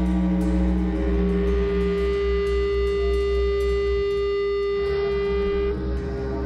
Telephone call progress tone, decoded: dial tone

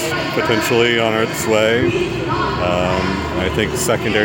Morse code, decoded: none

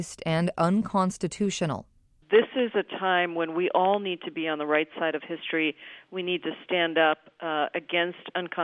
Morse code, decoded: none

